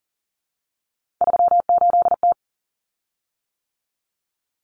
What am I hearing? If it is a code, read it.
Morse code, heard as a keyed tone, about 40 wpm, 700 Hz. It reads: 38T
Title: Common Cut Numbers - 3-Digit 40wpm